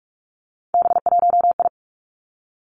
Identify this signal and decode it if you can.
Morse code, heard as a keyed tone, about 46 wpm, 700 Hz. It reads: B1I
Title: Three Challenging Letter-Number Combinations 45wpm